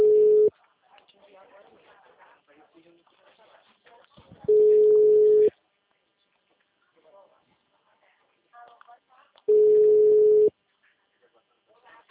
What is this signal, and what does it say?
Ringback tone (Europe, 425 Hz), with speech.